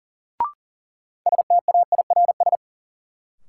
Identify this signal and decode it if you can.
Morse code, heard as a keyed tone, about 40 wpm, 700 Hz. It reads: STAIRS
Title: Sets of 5 Words Encore - New Word Drill - 40wpm